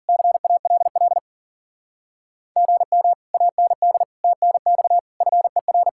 Morse code, heard as a keyed tone, about 52 wpm, 700 Hz. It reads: CALL GM AND TNX FER